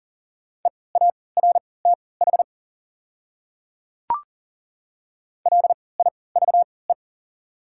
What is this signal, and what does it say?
Morse code, heard as a keyed tone, about 40 wpm, spaced as for 23 wpm, 700 Hz. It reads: EARTH LIVE
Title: 3x Character Spacing - Top 200 Words - 40wpm